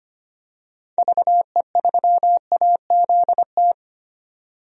Morse code, heard as a keyed tone, about 25 wpm, 700 Hz. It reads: VE3AZT
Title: DXCC-100 Call Signs - 6 Characters - 25wpm